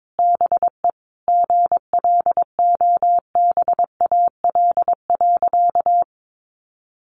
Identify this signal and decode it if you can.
Morse code, heard as a keyed tone, about 22 wpm, 700 Hz. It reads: BE GLOBAL.
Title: Sentences from Top 1200 Words 22wpm